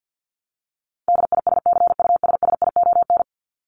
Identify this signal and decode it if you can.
Morse code, heard as a keyed tone, about 50 wpm, 700 Hz. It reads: DISCUSSION
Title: Top 1001-1250 Words 50wpm